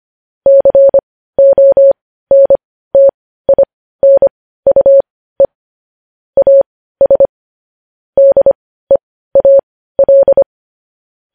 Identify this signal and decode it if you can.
Morse code, heard as a keyed tone, about 25 wpm, spaced as for 15 wpm, 550 Hz. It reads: CONTINUE AS DEAL
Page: news headlines in morse code at 15 wpm